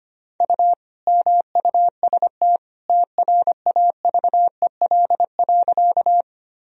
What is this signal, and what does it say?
Morse code, heard as a keyed tone, about 25 wpm, 700 Hz. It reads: U MUST TRAVEL.
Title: Sentences from Top 400 Words and Abbreviations 25wpm